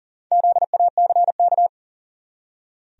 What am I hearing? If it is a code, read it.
Morse code, heard as a keyed tone, about 40 wpm, 700 Hz. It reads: ZACK
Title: USA Common Names - 40wpm